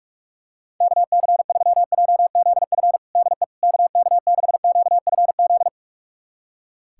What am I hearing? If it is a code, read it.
Morse code, heard as a keyed tone, about 41 wpm, 700 Hz. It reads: KC3JZF DE KK6QFZ
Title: USA Callsign DE Callsign 45wpm